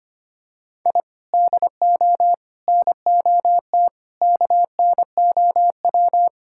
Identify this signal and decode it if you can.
Morse code, heard as a keyed tone, about 25 wpm, 700 Hz. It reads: I DO NOT KNOW
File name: practice / 25wpm / Sentences with 4-Word Phrases - 25wpm